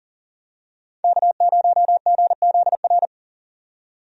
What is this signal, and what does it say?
Morse code, heard as a keyed tone, about 40 wpm, 700 Hz. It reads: K0GZR
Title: USA Call Signs - 5 Characters - 40wpm